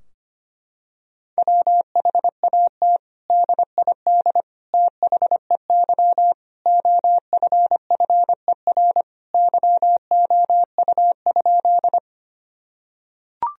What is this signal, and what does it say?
Morse code, heard as a keyed tone, about 25 wpm, 700 Hz. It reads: WHAT DID THEY OFFER YOU?
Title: Sentences from Top 1000 Words - Review Part 4 - 25wpm